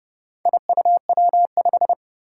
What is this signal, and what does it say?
Morse code, heard as a keyed tone, about 30 wpm, 700 Hz. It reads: IUW5
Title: Four Challenging Letter-Number Combinations 30wpm